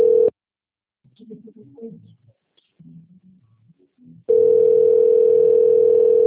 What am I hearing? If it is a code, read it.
Ringback tone (North America), 440 + 480 Hz.